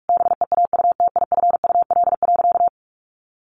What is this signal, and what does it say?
Morse code, heard as a keyed tone, about 44 wpm, 700 Hz. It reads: BEAUTIFUL.
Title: Sentences from Top 900 Words 45wpm